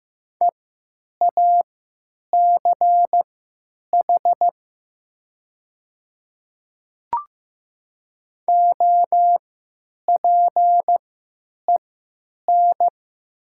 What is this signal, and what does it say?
Morse code, heard as a keyed tone, about 15 wpm, spaced as for 9 wpm, 700 Hz. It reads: EACH OPEN
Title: 3x Character Spacing - Top 300 Words - 15wpm